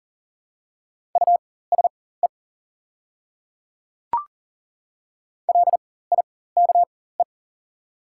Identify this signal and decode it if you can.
Morse code, heard as a keyed tone, about 40 wpm, spaced as for 19 wpm, 700 Hz. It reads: USE LIKE
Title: 4x Character Spacing - Top 100 Words - 40wpm